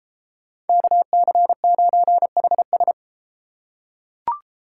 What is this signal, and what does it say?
Morse code, heard as a keyed tone, about 32 wpm, 700 Hz. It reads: KC9HS